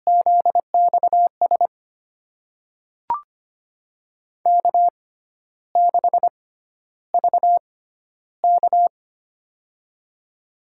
Morse code, heard as a keyed tone, about 25 wpm, spaced as for 12 wpm, 700 Hz. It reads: ZXS K6VK
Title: USA Call Signs - 6x Character Spacing - 25wpm